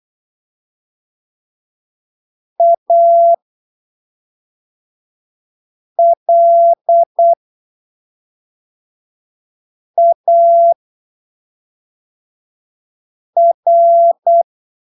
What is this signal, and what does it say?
Morse code, heard as a keyed tone, about 8 wpm, spaced as for 3 wpm, 680 Hz. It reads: ALAR
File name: texto 24 08ppm